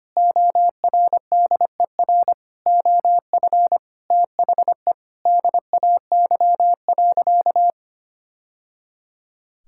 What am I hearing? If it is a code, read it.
Morse code, heard as a keyed tone, about 25 wpm, 700 Hz. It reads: ORDER OF THE DAY.
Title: Sentences from Top 1000 Words - Review Part 5 - 25wpm